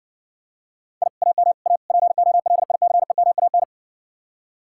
Morse code, heard as a keyed tone, about 60 wpm, 700 Hz. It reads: I AM A POLICEMAN